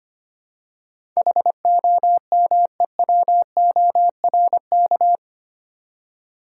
Morse code, heard as a keyed tone, about 25 wpm, 700 Hz. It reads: HOMEWORK
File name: 8 Letter Words - First 3 Letters Given - 25wpm